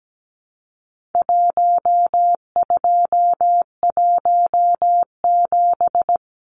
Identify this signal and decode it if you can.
Morse code, heard as a keyed tone, about 17 wpm, 700 Hz. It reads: 1217
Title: Numbers - Contest Style - 4-Digit 17wpm